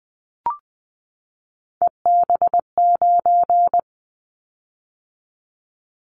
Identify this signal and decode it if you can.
Morse code, heard as a keyed tone, about 20 wpm, 700 Hz. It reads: EB9
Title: Callsign Prefixes for Top 100 DXCC - 20wpm